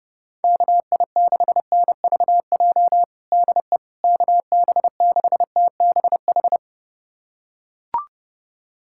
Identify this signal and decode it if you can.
Morse code, heard as a keyed tone, about 30 wpm, 700 Hz. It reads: KI6NVJ DE KB6TBH
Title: USA Callsign DE Callsign - 6 Characters - 30wpm